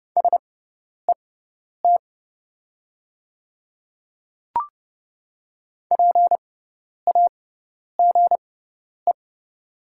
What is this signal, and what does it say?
Morse code, heard as a keyed tone, about 30 wpm, spaced as for 10 wpm, 700 Hz. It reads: SET PAGE
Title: 6x Character Spacing - Top 300 Words - 30wpm